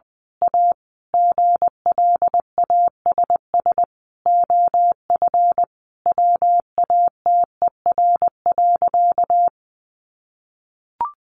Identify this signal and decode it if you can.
Morse code, heard as a keyed tone, about 20 wpm, 700 Hz. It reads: A GLASS OF WATER.
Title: Sentences from Top 1300 Words 20wpm